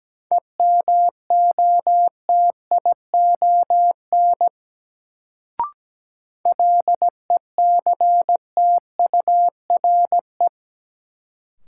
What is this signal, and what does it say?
Morse code, heard as a keyed tone, about 17 wpm, 700 Hz. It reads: EMOTION LECTURE
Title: Sentences from Top 1600 Words - New Word Drill - 17wpm